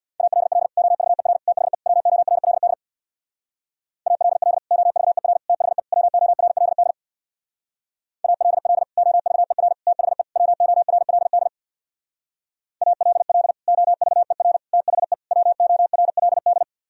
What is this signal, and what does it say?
Morse code, heard as a keyed tone, about 64 wpm, 700 Hz. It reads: ALL OVER THE WORLD ALL OVER THE WORLD ALL OVER THE WORLD ALL OVER THE WORLD